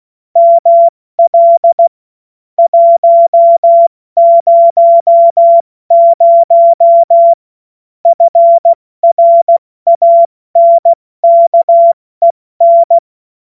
Morse code, heard as a keyed tone, about 16 wpm, 680 Hz. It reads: ML 100 FRANKEN
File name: Morsekurs / Witze / Witz 31. August 2021, 70 F80